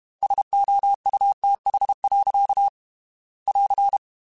Morse code, heard as a keyed tone, about 32 wpm, 800 Hz. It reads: SOUTH. <AR>
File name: Weather Sets / Weather Sets-mp3s / Weather 4 21 @30